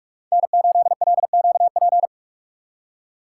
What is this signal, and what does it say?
Morse code, heard as a keyed tone, about 43 wpm, 700 Hz. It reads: N8LQP
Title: USA Call Signs - 5 Characters - 2x Character Spacing - 45wpm